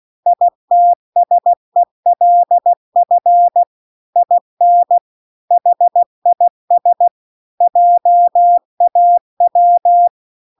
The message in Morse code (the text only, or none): ITSELF IN HIS JAW